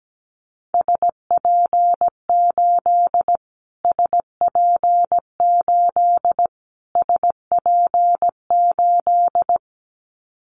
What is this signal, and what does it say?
Morse code, heard as a keyed tone, about 17 wpm, 700 Hz. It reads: SP8 SP8 SP8
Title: Callsign Prefixes for Top 25 DXCC - Repeated 3x - 17wpm